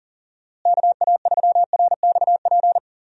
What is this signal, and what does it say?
Morse code, heard as a keyed tone, about 40 wpm, 700 Hz. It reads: KA3RXP
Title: USA Call Signs - 6 Characters - Repeated 3x - 40wpm